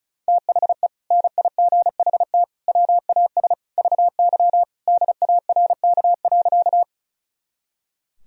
Morse code, heard as a keyed tone, about 34 wpm, 700 Hz. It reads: THE NIGHT WAS VY DARK.
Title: Sentences from Top 500 Words and Abbreviations 35wpm